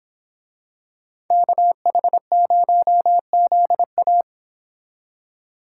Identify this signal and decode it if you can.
Morse code, heard as a keyed tone, about 26 wpm, 700 Hz. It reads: KH0ZA
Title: Speed Racing - USA 2x2 Call Signs 22wpm